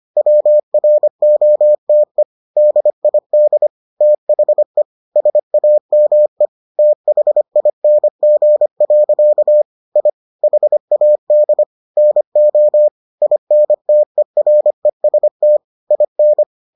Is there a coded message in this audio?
Yes: Morse code from 0.2 to 16.4 s, heard as a keyed tone, about 25 wpm, 600 Hz. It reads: WROTE DID THE SAME THING. I HAD NO INTEREST IN